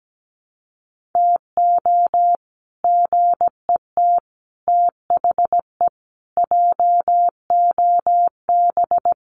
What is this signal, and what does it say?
Morse code, heard as a keyed tone, about 17 wpm, 700 Hz. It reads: TO GET THE JOB
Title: Sets of 5 Words Encore2 17wpm